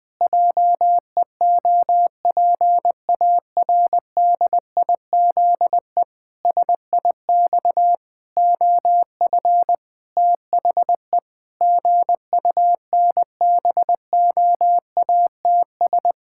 Morse code, heard as a keyed tone, about 21 wpm, 700 Hz. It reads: JEOPARDIZE SIX OF THE GUNBOATS
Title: Pangram Challenge v2 - 20wpm